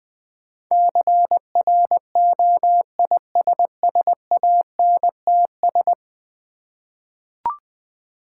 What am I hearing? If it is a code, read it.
Morse code, heard as a keyed tone, about 20 wpm, 700 Hz. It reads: CROISSANTS
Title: Difficult Words with Hints - 20wpm